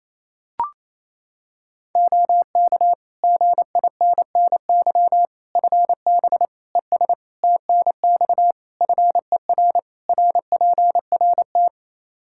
Morse code, heard as a keyed tone, about 28 wpm, 700 Hz. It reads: OK GINNY FB ES TNX FER RPRT
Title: Practice QSO - Exchange 2 - Info - 28wpm